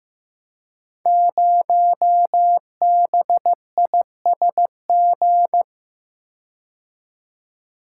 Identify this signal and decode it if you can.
Morse code, heard as a keyed tone, about 15 wpm, 700 Hz. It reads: 0BISG
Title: Code Groups - 5 Letter:Numbers - 15wpm